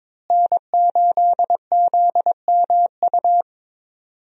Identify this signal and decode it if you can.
Morse code, heard as a keyed tone, about 22 wpm, 700 Hz. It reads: N8ZMU
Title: USA Call Signs - 5 Characters - Repeated 3x - 22wpm